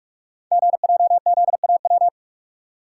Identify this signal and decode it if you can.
Morse code, heard as a keyed tone, about 44 wpm, 700 Hz. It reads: GJZAW
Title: Code Groups - 5 Letters - 45wpm